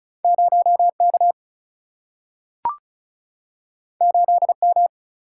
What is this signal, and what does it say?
Morse code, heard as a keyed tone, about 36 wpm, 700 Hz. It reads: 0K 8M
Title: Single-Letter Single-Number 35wpm